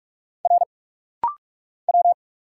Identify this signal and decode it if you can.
Morse code, heard as a keyed tone, about 46 wpm, 700 Hz. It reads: R W
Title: Single Letters - Rapid-Fire 45wpm